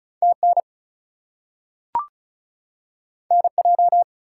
Morse code, heard as a keyed tone, about 36 wpm, 700 Hz. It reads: TN NJ
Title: US States and Canadian Provinces 35wpm